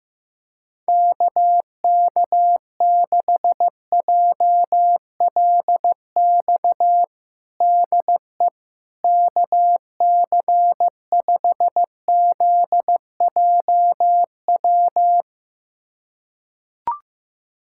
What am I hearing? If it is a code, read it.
Morse code, heard as a keyed tone, about 15 wpm, 700 Hz. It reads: KK6JLX DE KC5ZJW
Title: USA Callsign DE Callsign 15wpm